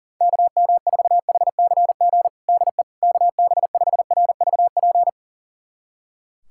Morse code, heard as a keyed tone, about 40 wpm, 700 Hz. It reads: KM4HCG DE KB5RVP
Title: USA Callsign DE Callsign 40wpm